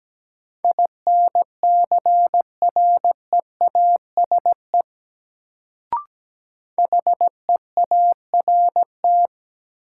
Morse code, heard as a keyed tone, about 17 wpm, 700 Hz. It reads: INCREASE HEART
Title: Sentences from Top 1200 Words - New Word Drill Alt - 17wpm